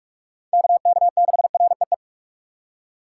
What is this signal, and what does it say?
Morse code, heard as a keyed tone, about 49 wpm, 700 Hz. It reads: KK6REE